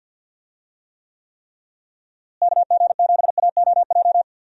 Morse code, heard as a keyed tone, about 46 wpm, 700 Hz. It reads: KG7AOJ
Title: USA Call Signs - 6 Characters - Repeated 3x - 50wpm